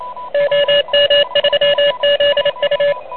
Morse code, heard as a keyed tone, about 27 wpm, 590 Hz, with noise. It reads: OM3ZU
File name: truSDX / MWC29.8